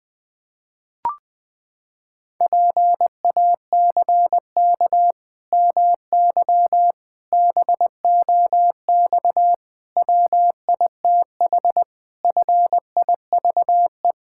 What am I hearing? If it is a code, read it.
Morse code, heard as a keyed tone, about 21 wpm, 700 Hz. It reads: PACK MY BOX WITH FIVE
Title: Pangram Challenge v2 - 20wpm